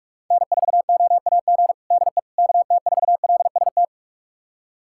Morse code, heard as a keyed tone, about 45 wpm, 700 Hz. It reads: N4OAG DE KT4LST
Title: USA Callsign DE Callsign 45wpm